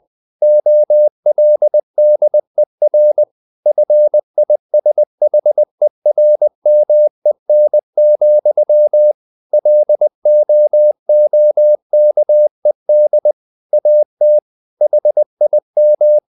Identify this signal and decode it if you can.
Morse code, heard as a keyed tone, about 20 wpm, 600 Hz. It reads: OLDER FISHERMEN, LOOKED AT HIM
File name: the old man and the sea / the old man and the sea part 1 20wpm